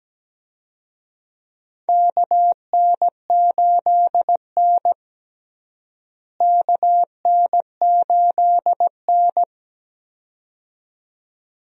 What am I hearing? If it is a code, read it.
Morse code, heard as a keyed tone, about 17 wpm, 700 Hz. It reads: KN8N KN8N
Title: USA Call Signs - 4 Characters - Repeated 2x - 17wpm